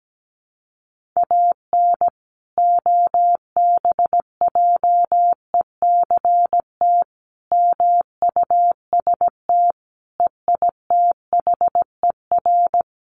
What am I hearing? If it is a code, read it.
Morse code, heard as a keyed tone, about 17 wpm, 700 Hz. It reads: AN OBJECT MUST EITHER
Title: Sentences from Top 1400 Words 17wpm